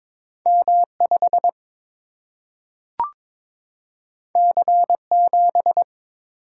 Morse code, heard as a keyed tone, about 22 wpm, 700 Hz. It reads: M 5 C 7
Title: Single-Letter Single-Number 22wpm